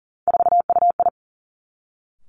Morse code, heard as a keyed tone, about 40 wpm, 700 Hz. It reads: 4UI
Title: Three Challenging Letter-Number Combinations 40wpm